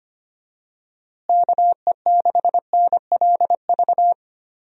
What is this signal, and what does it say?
Morse code, heard as a keyed tone, about 25 wpm, 700 Hz. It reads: KE6NLV
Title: USA Call Signs - 6 Characters - Repeated 3x - 25wpm